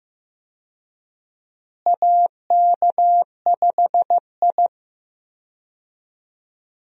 Morse code, heard as a keyed tone, about 15 wpm, 700 Hz. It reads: AK5I
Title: USA Call Signs - 4 Characters - Repeated 3x - 15wpm